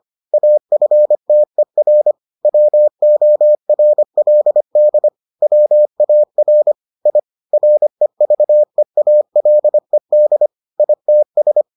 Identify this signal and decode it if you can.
Morse code, heard as a keyed tone, about 25 wpm, 600 Hz. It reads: AFTER WORLD WAR I REVEALED ITS